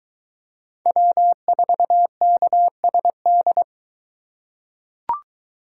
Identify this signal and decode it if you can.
Morse code, heard as a keyed tone, about 22 wpm, 700 Hz. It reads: W4KSD